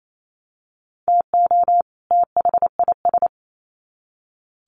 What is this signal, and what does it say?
Morse code, heard as a keyed tone, about 28 wpm, 700 Hz. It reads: TO THIS
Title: Sets of 2 Words 28wpm